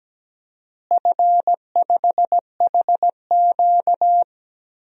Morse code, heard as a keyed tone, about 17 wpm, 700 Hz. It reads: F5HQ
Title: DXCC-100 Call Signs - 4 Characters - Repeated 2x - 17wpm